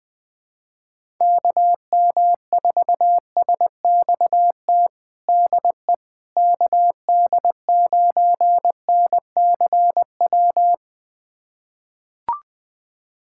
Morse code, heard as a keyed tone, about 19 wpm, 700 Hz. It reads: KM4SXT DE KD9NCW
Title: USA Callsign DE Callsign - 6 Characters - 20wpm